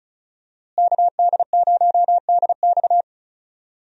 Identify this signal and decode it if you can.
Morse code, heard as a keyed tone, about 34 wpm, 700 Hz. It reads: KD0DX
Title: USA Call Signs - 5 Characters - Repeated 2x - 35wpm